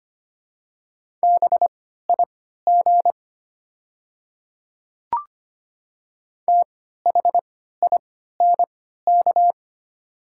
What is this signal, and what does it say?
Morse code, heard as a keyed tone, about 25 wpm, spaced as for 14 wpm, 700 Hz. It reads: BIG THINK
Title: 3x Character Spacing - Top 200 Words - 25wpm